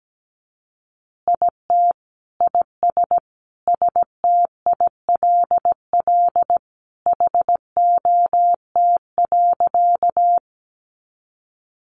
Morse code, heard as a keyed tone, about 17 wpm, 700 Hz. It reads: IT IS STILL HOT.
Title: Easy Phrases 17wpm